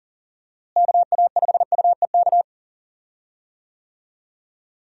Morse code, heard as a keyed tone, about 40 wpm, 700 Hz. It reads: KA5UEK
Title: USA Call Signs - 6 Characters - Repeated 3x - 40wpm